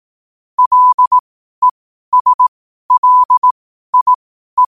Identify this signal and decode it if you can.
Morse code, heard as a keyed tone, about 18 wpm, spaced as for 12 wpm, 1000 Hz. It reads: LESLIE
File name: qotdfiles / 12wpm